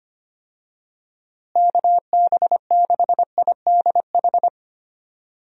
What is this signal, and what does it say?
Morse code, heard as a keyed tone, about 25 wpm, 700 Hz. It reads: KB6IDH
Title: USA Call Signs - 6 Characters - Repeated 2x - 25wpm